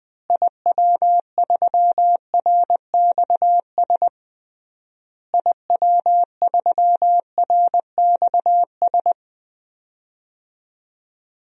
Morse code, heard as a keyed tone, about 20 wpm, 700 Hz. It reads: IW3RXS IW3RXS
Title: DXCC-100 Call Signs - 6 Characters - Repeated 3x - 20wpm